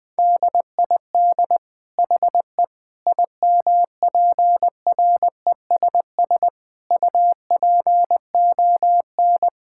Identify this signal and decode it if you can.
Morse code, heard as a keyed tone, about 20 wpm, 700 Hz. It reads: DID HE IMPRESS UPON
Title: Sentences from Top 1600 Words 20wpm